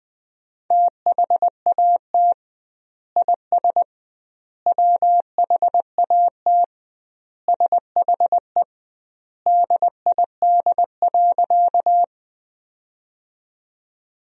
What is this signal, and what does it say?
Morse code, heard as a keyed tone, about 20 wpm, 700 Hz. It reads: THAT IS WHAT SHE DID.
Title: 2x Word-Spacing - Sentences from Top 100 Words - 20wpm